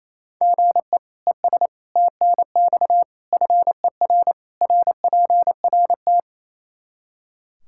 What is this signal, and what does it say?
Morse code, heard as a keyed tone, about 28 wpm, 700 Hz. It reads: GE ES TNX FER RPRT
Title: Practice QSO - Exchange 1 - Beginning - 28wpm